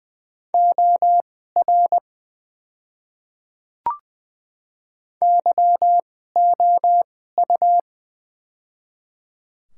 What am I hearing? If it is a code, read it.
Morse code, heard as a keyed tone, about 20 wpm, spaced as for 14 wpm, 700 Hz. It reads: OR YOU
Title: 2x Character Spacing - Top 100 Words - 20wpm